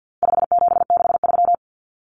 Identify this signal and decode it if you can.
Morse code, heard as a keyed tone, about 50 wpm, 700 Hz. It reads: 5763
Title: Common Cut Numbers - 4-Digit 50wpm